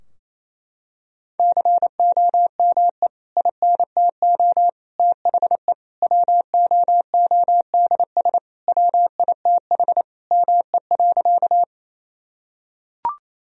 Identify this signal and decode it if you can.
Morse code, heard as a keyed tone, about 28 wpm, 700 Hz. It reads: COME INTO THE WOODS WITH ME.
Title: Sentences from Top 1000 Words - Review Part 2 - 28wpm